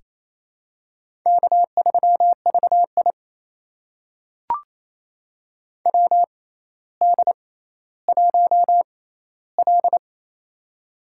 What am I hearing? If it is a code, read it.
Morse code, heard as a keyed tone, about 28 wpm, spaced as for 14 wpm, 700 Hz. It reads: K3VI WD1L
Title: USA Call Signs - 4 Characters - 6x Character Spacing - 28wpm